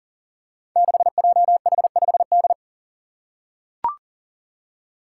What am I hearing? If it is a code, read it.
Morse code, heard as a keyed tone, about 39 wpm, 700 Hz. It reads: 6JH5D
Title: Five Challenging Letter-Number Combinations 40wpm